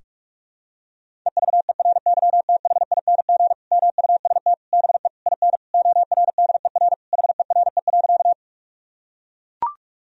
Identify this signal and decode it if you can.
Morse code, heard as a keyed tone, about 41 wpm, 700 Hz. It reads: EVERYTHING MUST BE IN ORDER HERE.
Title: Sentences from Top 1000 Words - Review Part 2 - 45wpm